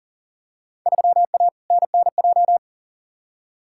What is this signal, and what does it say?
Morse code, heard as a keyed tone, about 39 wpm, 700 Hz. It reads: 3A NNJ
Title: Field Day Exchanges - 40wpm